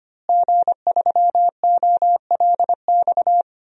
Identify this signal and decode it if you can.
Morse code, heard as a keyed tone, about 25 wpm, 700 Hz. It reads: G3OLX
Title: DXCC-100 Call Signs - 5 Characters - Repeated 3x - 25wpm